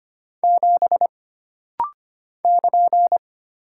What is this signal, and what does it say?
Morse code, heard as a keyed tone, about 25 wpm, 700 Hz. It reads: 7 <KN>
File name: Single Character (Letter:Number:Punctuation:Prosign) - Rapid-Fire - 25wpm